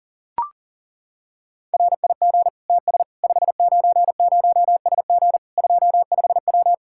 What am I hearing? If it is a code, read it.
Morse code, heard as a keyed tone, about 40 wpm, 700 Hz. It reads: RIG TS 590SG 25W
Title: Practice QSO - Exchange 2 - Info - 40wpm